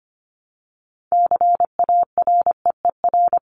Morse code, heard as a keyed tone, about 25 wpm, 700 Hz. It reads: CAREER
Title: Top 751-1000 Words 25wpm